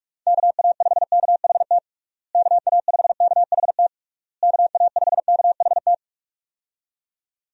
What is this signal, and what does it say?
Morse code, heard as a keyed tone, about 41 wpm, 700 Hz. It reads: KA5KHT KA5KHT KA5KHT
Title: USA Call Signs - 6 Characters - Repeated 3x - 45wpm